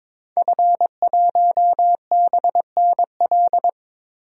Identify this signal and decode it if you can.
Morse code, heard as a keyed tone, about 22 wpm, 700 Hz. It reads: F1BNL
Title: DXCC-100 Call Signs - 5 Characters - Repeated 2x - 22wpm